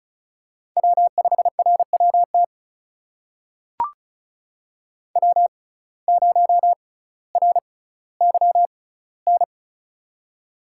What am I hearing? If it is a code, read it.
Morse code, heard as a keyed tone, about 36 wpm, spaced as for 18 wpm, 700 Hz. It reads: W5RWT W0RYN
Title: USA Call Signs - 5 Characters - 6x Character Spacing - 35wpm